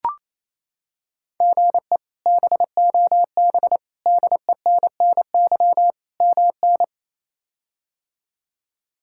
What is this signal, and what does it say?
Morse code, heard as a keyed tone, about 28 wpm, 700 Hz. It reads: GE BOB DENNY MN
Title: K1USN SST QSO Practice - 28wpm